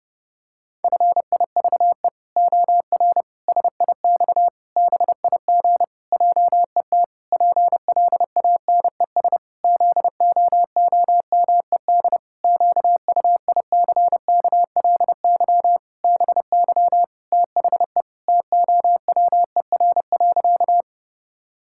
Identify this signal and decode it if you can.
Morse code, heard as a keyed tone, about 30 wpm, 700 Hz. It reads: FIVE OR SIX BIG JET PLANES ZOOMED QUICKLY BY THE TOWER.
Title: Pangram Challenge v2 - 30wpm